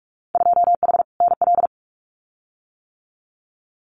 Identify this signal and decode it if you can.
Morse code, heard as a keyed tone, about 46 wpm, 700 Hz. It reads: 2H NL
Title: Winter Field Day Exchanges Encore 45wpm